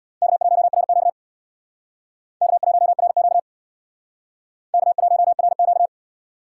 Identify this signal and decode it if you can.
Morse code, heard as a keyed tone, about 60 wpm, 700 Hz. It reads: K1RQ K1RQ K1RQ
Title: Speed Racing - USA Call Signs 50wpm